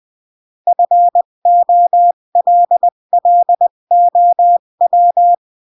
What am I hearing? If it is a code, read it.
Morse code, heard as a keyed tone, about 20 wpm, 700 Hz. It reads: FOLLOW
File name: CW / jodorowsky / 82 commandments 20-15